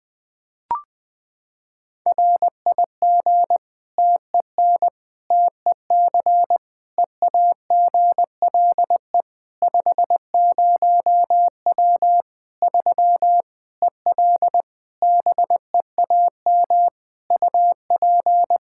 Morse code, heard as a keyed tone, about 20 wpm, 700 Hz. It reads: RIG TEN TEC EAGLE 50W 3 EL BEAM UP
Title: Practice QSO - Exchange 2 - Info - 20wpm